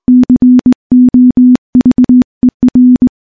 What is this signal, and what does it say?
Morse code, heard as a keyed tone, about 20 wpm, 250 Hz. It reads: COVER
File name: morse-training-20WPM